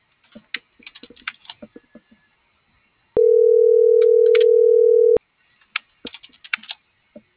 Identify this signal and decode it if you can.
Ringback tone (North America), 440 + 480 Hz.